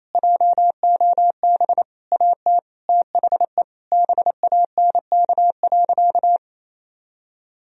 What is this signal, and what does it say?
Morse code, heard as a keyed tone, about 28 wpm, 700 Hz. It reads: JOB AT THE BANK.
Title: Sentences from Top 1000 Words - Review Part 4 - 28wpm